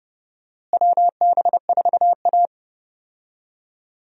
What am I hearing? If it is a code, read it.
Morse code, heard as a keyed tone, about 32 wpm, 700 Hz. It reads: WB4A
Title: USA Call Signs - 4 Characters - Repeated 3x - 30wpm